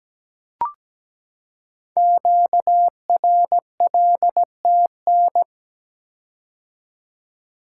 Morse code, heard as a keyed tone, about 17 wpm, 700 Hz. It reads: QRLTN
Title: Code Groups - 5 Letters - 17wpm